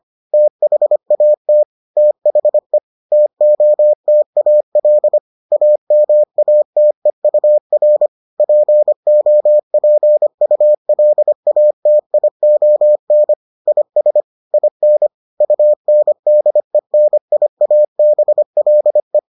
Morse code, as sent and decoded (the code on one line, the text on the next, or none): - .... .- - / - .... / - --- - .- .-.. / .- -- .- - . ..- .-. / .--. --- .--. ..- .-.. .- - .. --- -. / .. ... / .. -. / ..- -. -.. -. .. .- -... .-.. .
THAT THE TOTAL AMATEUR POPULATION IS IN UNDENIABLE